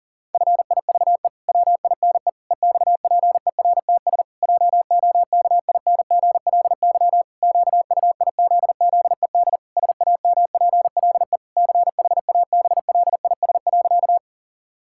Morse code, heard as a keyed tone, about 40 wpm, 700 Hz. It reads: FIVE WINE EXPERTS JOKINGLY QUIZZED SAMPLE CHABLIS.
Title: Pangram Challenge v2 - 40wpm